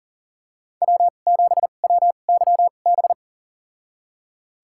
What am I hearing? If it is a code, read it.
Morse code, heard as a keyed tone, about 40 wpm, spaced as for 29 wpm, 700 Hz. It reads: W7WYB
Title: USA Call Signs - 2x Character Spacing - 40wpm